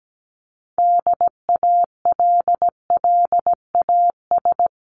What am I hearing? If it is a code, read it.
Morse code, heard as a keyed tone, about 17 wpm, 700 Hz. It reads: DALLAS
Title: CWT Exchange 17wpm